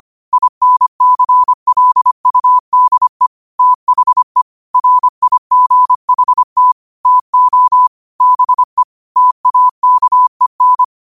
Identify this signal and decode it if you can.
Morse code, heard as a keyed tone, about 25 wpm, 1000 Hz. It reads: INCLUDE THE RIGHT TO BE TAKEN
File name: 25wpm